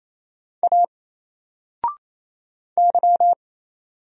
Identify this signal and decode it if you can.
Morse code, heard as a keyed tone, about 27 wpm, 700 Hz. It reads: A Y
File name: Single-Letter-Number-28wpm